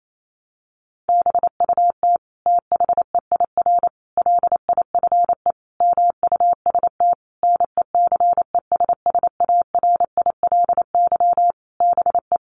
Morse code, heard as a keyed tone, about 28 wpm, 700 Hz. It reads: BUT THEIR LIFE MUST NECESSARILY BE
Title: Sentences from Top 1400 Words 28wpm